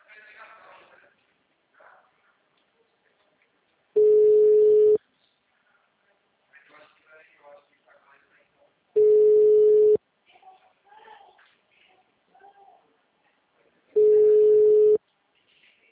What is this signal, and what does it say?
Ringback tone (Europe, 425 Hz), with speech.